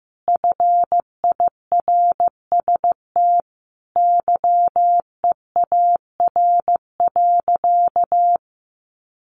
Morse code, heard as a keyed tone, about 15 wpm, 700 Hz. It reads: FIRST YEAR.
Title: Sentences from Top 1300 Words 15wpm